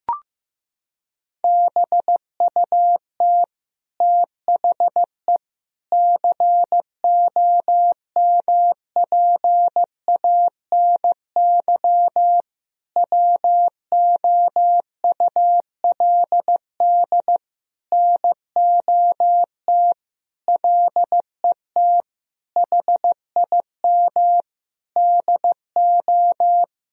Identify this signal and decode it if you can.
Morse code, heard as a keyed tone, about 15 wpm, 700 Hz. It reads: BUT THE COMPANY WOULD NOT LET HIM DO
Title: Sentences from Top 1000 Words - Review Part 3 - 15wpm